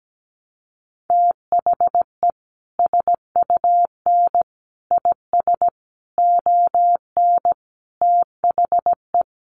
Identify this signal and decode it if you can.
Morse code, heard as a keyed tone, about 17 wpm, 700 Hz. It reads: THE SUN IS ON THE
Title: Sentences from Top 1300 Words 17wpm